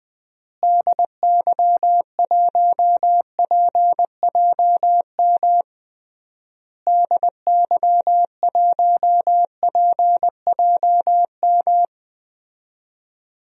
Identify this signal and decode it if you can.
Morse code, heard as a keyed tone, about 20 wpm, 700 Hz. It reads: DY1PJM DY1PJM
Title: DXCC-100 Call Signs - 6 Characters - Repeated 3x - 20wpm